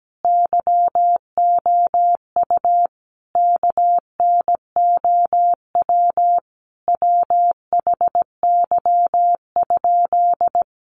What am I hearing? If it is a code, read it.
Morse code, heard as a keyed tone, about 17 wpm, 700 Hz. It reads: YOU KNOW WHY?
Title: Sentences from Top 200 Words 17wpm